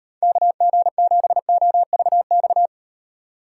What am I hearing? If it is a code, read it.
Morse code, heard as a keyed tone, about 38 wpm, 700 Hz. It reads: KG7OVX